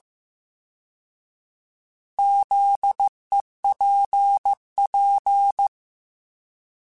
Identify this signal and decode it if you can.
Morse code, heard as a keyed tone, about 15 wpm, 800 Hz. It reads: ZEPP
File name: fromSMs / FromSMs mp3s / Antennas@13